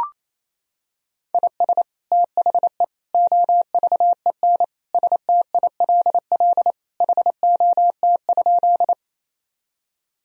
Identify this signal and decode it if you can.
Morse code, heard as a keyed tone, about 28 wpm, 700 Hz. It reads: IS THE OVEN STILL HOT?
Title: Sentences from Top 1000 Words - Review Part 4 - 28wpm